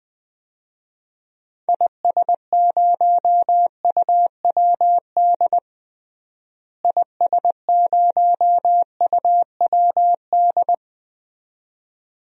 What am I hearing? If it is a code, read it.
Morse code, heard as a keyed tone, about 20 wpm, 700 Hz. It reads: IS0UWD IS0UWD
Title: DXCC-100 Call Signs - 6 Characters - Repeated 2x - 20wpm